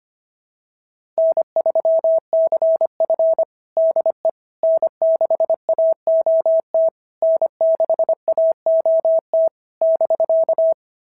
Morse code, heard as a keyed tone, about 25 wpm, 690 Hz. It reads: N3CF DE N6AOT N6AOT <BK>
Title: SOTA QSO Practice 25wpm